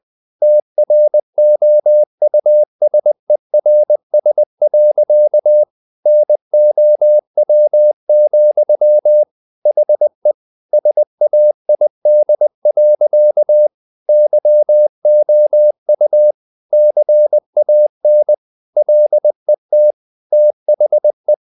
Morse code, heard as a keyed tone, about 20 wpm, 600 Hz. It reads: TROUSERS. NOW, HE SAID. YOU CAN LET THE
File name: the old man and the sea / the old man and the sea part 1 20wpm